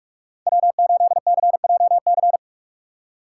Morse code, heard as a keyed tone, about 46 wpm, 700 Hz. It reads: W8CJC
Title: USA Call Signs - 5 Characters - 4x Character Spacing - 45wpm